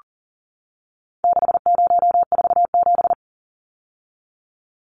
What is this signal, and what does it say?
Morse code, heard as a keyed tone, about 40 wpm, 700 Hz. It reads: 6047
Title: Numbers - 4-Digit 40wpm